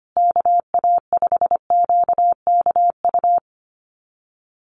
Morse code, heard as a keyed tone, about 25 wpm, 700 Hz. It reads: KA5QKU
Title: USA Call Signs Repeated 2x 25wpm